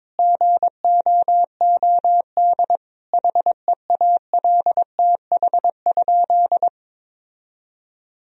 Morse code, heard as a keyed tone, about 22 wpm, 700 Hz. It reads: GOOD HEALTH?
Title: Sentences from Top 1000 Words - Review Part 3 - 22wpm